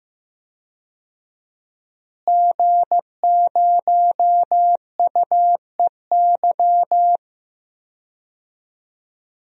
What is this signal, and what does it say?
Morse code, heard as a keyed tone, about 15 wpm, 700 Hz. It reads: G0UEY
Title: DXCC-100 Call Signs - 5 Characters - Repeated 2x - 15wpm